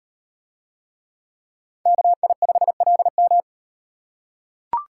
Morse code, heard as a keyed tone, about 38 wpm, 700 Hz. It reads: KI5LM